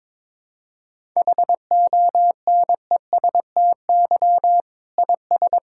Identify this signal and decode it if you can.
Morse code, heard as a keyed tone, about 22 wpm, 700 Hz. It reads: HONESTY IS
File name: Difficult Sentences with Hints - 22wpm